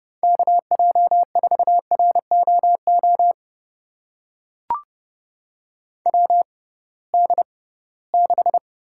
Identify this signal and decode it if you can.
Morse code, heard as a keyed tone, about 32 wpm, spaced as for 20 wpm, 700 Hz. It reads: KJ4ROO WD6
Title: USA Call Signs - 6 Characters - 6x Character Spacing - 30wpm